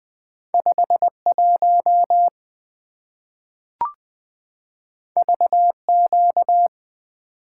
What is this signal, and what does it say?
Morse code, heard as a keyed tone, about 20 wpm, 700 Hz. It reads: 5 1 V Q
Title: Two Letter-Number Combinations - 20wpm